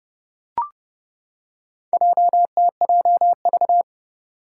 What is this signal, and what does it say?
Morse code, heard as a keyed tone, about 30 wpm, 700 Hz. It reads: JTJV
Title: Four Challenging Letter-Number Combinations 30wpm